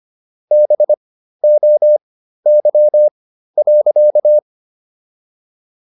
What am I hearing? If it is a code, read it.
Morse code, heard as a keyed tone, about 25 wpm, spaced as for 13 wpm, 600 Hz. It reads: BOY.